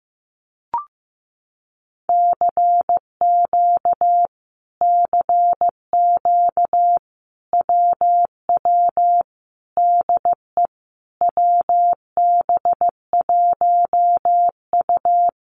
Morse code, heard as a keyed tone, about 15 wpm, 700 Hz. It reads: CQ CQ WW DE WB1U
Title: Common QSO Phrases Encore 15wpm